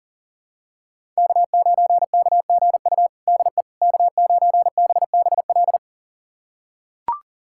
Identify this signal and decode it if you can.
Morse code, heard as a keyed tone, about 40 wpm, 700 Hz. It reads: K9KGU DE K9BBL
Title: USA Callsign DE Callsign - 5 Characters - 40wpm